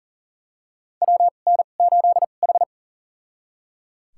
Morse code, heard as a keyed tone, about 37 wpm, spaced as for 28 wpm, 700 Hz. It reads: WN8H
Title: USA Call Signs - 4 Characters - 2x Character Spacing - 40wpm